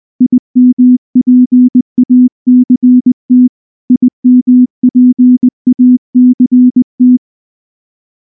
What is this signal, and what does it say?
Morse code, heard as a keyed tone, about 20 wpm, 250 Hz. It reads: IMPACT IMPACT
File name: morse / morse-training-20WPM